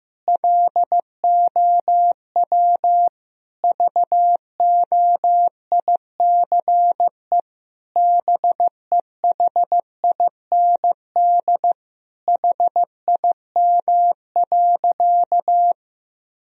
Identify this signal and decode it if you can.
Morse code, heard as a keyed tone, about 15 wpm, 700 Hz. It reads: LOW VOICE BEHIND HIM.
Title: Sentences from Top 1000 Words - Review Part 1 - 15wpm